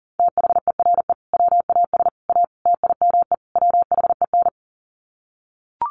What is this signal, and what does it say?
Morse code, heard as a keyed tone, about 40 wpm, 700 Hz. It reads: THERE WAS A TIME WHEN
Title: Sets of 5 Words 40wpm